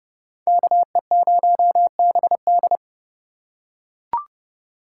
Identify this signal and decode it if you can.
Morse code, heard as a keyed tone, about 28 wpm, 700 Hz. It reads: KE0BD